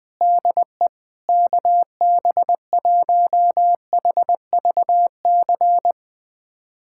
Morse code, heard as a keyed tone, about 20 wpm, 700 Hz. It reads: DE KB1HVC
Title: USA Callsign DE Callsign 20wpm